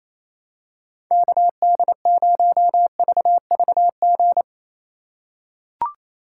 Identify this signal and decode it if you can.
Morse code, heard as a keyed tone, about 27 wpm, 700 Hz. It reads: KD0VVG
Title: USA Call Signs - 6 Characters - 3x Character Spacing - 28wpm